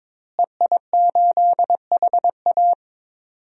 Morse code, heard as a keyed tone, about 21 wpm, 700 Hz. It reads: EI8HA
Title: DXCC-100 Call Signs - 5 Characters - Repeated 3x - 22wpm